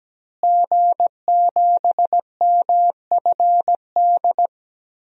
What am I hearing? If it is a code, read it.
Morse code, heard as a keyed tone, about 17 wpm, 700 Hz. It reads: G7MFD
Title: DXCC-100 Call Signs - 5 Characters - Repeated 3x - 17wpm